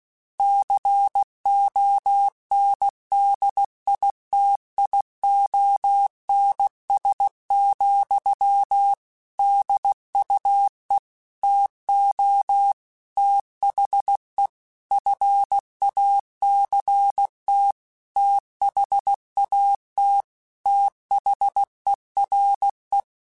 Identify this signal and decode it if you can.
Morse code, heard as a keyed tone, about 16 wpm, 800 Hz. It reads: CONDITIONS, DUE TO THE FACT THAT THERE
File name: Technical Subjects / Technical Subjects-mp3s / AND Gate @15